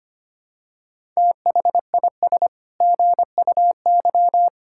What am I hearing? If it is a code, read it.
Morse code, heard as a keyed tone, about 25 wpm, 700 Hz. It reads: THIS GUY
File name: Sets of 9 Words - 25wpm